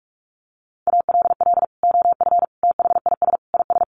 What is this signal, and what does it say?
Morse code, heard as a keyed tone, about 44 wpm, 700 Hz. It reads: ALL OF THIS IS
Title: Sentences from Top 1100 Words 45wpm